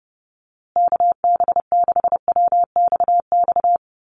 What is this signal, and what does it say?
Morse code, heard as a keyed tone, about 30 wpm, 700 Hz. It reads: KB6WXX
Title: USA Call Signs Repeated 2x 30wpm